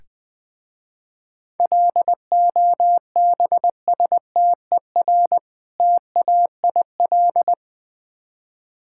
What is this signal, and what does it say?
Morse code, heard as a keyed tone, about 20 wpm, 700 Hz. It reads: LOBSTER TAIL
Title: Easy Words with Hints - 20wpm